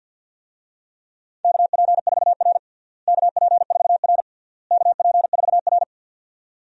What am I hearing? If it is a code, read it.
Morse code, heard as a keyed tone, about 50 wpm, 700 Hz. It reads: KP4R KP4R KP4R
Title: DXCC-100 Call Signs - 4 Characters - Repeated 3x - 50wpm